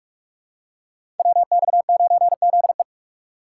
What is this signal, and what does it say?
Morse code, heard as a keyed tone, about 44 wpm, 700 Hz. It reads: WX9ZE
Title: Code Groups - 5 Letter:Numbers - 45wpm